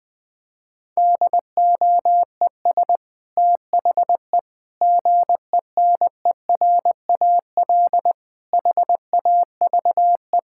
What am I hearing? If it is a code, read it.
Morse code, heard as a keyed tone, about 21 wpm, 700 Hz. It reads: DOES THE GENERAL HAVE
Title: Sentences from Top 1000 Words - Review Part 6 - 20wpm